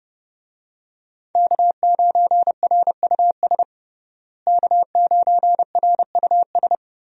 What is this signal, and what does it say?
Morse code, heard as a keyed tone, about 32 wpm, 700 Hz. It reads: K9RUS K9RUS
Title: USA Call Signs - 5 Characters - Repeated 3x - 30wpm